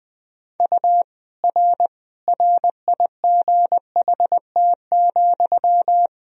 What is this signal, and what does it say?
Morse code, heard as a keyed tone, about 20 wpm, 700 Hz. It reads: U R RIGHT,
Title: Sentences from Top 300 Words and Abbreviations 20wpm